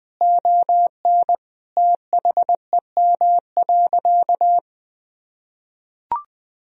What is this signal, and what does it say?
Morse code, heard as a keyed tone, about 20 wpm, 700 Hz. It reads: ON THEM.
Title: Sentences from Top 1000 Words - Review Part 1 - 20wpm